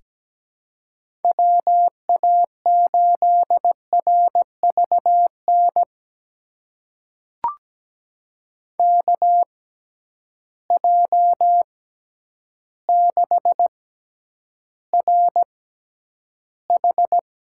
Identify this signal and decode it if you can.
Morse code, heard as a keyed tone, about 17 wpm, spaced as for 9 wpm, 700 Hz. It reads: WA8RVN KJ6RH
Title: USA Call Signs - 6 Characters - 6x Character Spacing - 17wpm